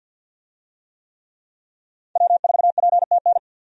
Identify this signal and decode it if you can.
Morse code, heard as a keyed tone, about 51 wpm, 700 Hz. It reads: W4PTN
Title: USA Call Signs - 5 Characters - 50wpm